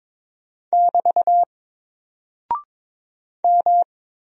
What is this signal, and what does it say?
Morse code, heard as a keyed tone, about 22 wpm, 700 Hz. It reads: <BT> M
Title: Single Character (Letter:Number:Punctuation:Prosign) Alt 22wpm